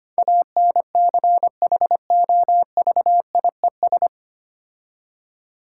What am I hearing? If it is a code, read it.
Morse code, heard as a keyed tone, about 25 wpm, 690 Hz. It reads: ANCHOVIES
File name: practice / 25wpm / Difficult Words with Hints - 25wpm